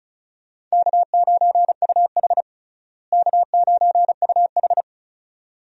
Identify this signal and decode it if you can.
Morse code, heard as a keyed tone, about 34 wpm, 700 Hz. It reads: K9UH K9UH
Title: USA Call Signs - 4 Characters - Repeated 2x - 35wpm